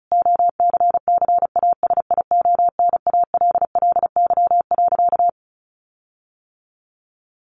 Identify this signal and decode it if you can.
Morse code, heard as a keyed tone, about 34 wpm, 700 Hz. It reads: OCCASIONALLY.
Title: Sentences from Top 1300 Words 35wpm